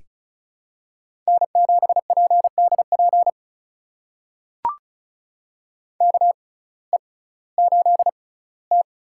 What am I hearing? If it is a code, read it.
Morse code, heard as a keyed tone, about 34 wpm, spaced as for 19 wpm, 700 Hz. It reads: N7PDP KE8T
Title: USA Call Signs - 5 Characters - 6x Character Spacing - 35wpm